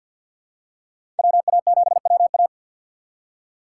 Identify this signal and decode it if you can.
Morse code, heard as a keyed tone, about 52 wpm, 690 Hz. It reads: WA7WA
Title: USA Call Signs - 5 Characters - 2x Character Spacing - 50wpm